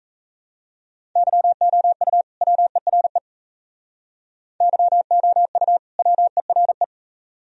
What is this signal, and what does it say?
Morse code, heard as a keyed tone, about 35 wpm, 700 Hz. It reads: YOU WERE YOU WERE
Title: Speed-Racing - Sets of 2 Words 28wpm